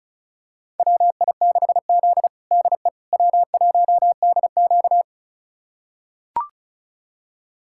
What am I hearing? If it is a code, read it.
Morse code, heard as a keyed tone, about 34 wpm, 700 Hz. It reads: WI6Z DE W1DQ
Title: USA Callsign DE Callsign - 4 Characters - 35wpm